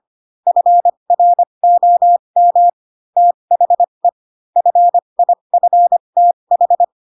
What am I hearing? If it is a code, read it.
Morse code, heard as a keyed tone, about 25 wpm, spaced as for 22 wpm, 700 Hz. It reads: FROM THE FIFTH
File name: Pinocchio / Pinocchio ch7 18 25 ik1hge